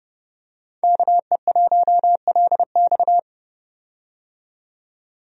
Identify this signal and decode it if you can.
Morse code, heard as a keyed tone, about 30 wpm, 700 Hz. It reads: KE1LX
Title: USA Call Signs - 5 Characters - 30wpm